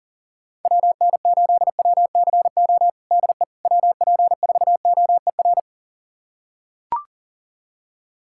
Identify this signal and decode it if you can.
Morse code, heard as a keyed tone, about 40 wpm, 700 Hz. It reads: WN8WCO DE WP4OER
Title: USA Callsign DE Callsign - 6 Characters - 40wpm